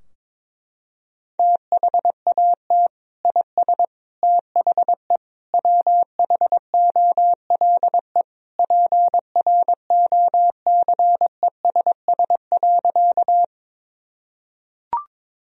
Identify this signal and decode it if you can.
Morse code, heard as a keyed tone, about 21 wpm, 700 Hz. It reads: THAT IS THE WHOLE PROCESS.
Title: Sentences from Top 1000 Words - Review Part 3 - 22wpm